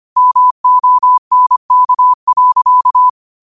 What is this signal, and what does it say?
Morse code, heard as a keyed tone, about 25 wpm, 1000 Hz. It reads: MONK.
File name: qotdfiles / 25wpm